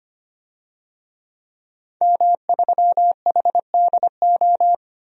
Morse code, heard as a keyed tone, about 25 wpm, 700 Hz. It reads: M3HDO
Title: DXCC-100 Call Signs - 5 Characters - 25wpm